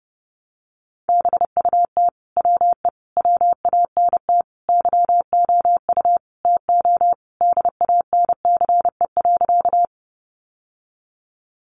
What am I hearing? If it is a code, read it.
Morse code, heard as a keyed tone, about 30 wpm, 700 Hz. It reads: BUT WE WANT YOU TO DANCE.
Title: Sentences from Top 700 Words 30wpm